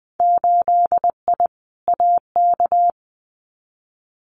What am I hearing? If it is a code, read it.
Morse code, heard as a keyed tone, about 20 wpm, 700 Hz. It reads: 8I AK
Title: Winter Field Day Exchanges Encore 20wpm